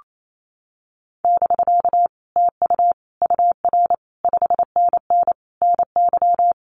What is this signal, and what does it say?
Morse code, heard as a keyed tone, about 28 wpm, 700 Hz. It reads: <BK> TU UR 5NN NY
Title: Common QSO Phrases Encore 28wpm